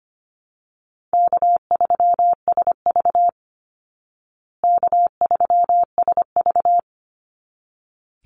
Morse code, heard as a keyed tone, about 25 wpm, 700 Hz. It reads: K3SV K3SV
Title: Easy Call Signs Repeated 2x - 25wpm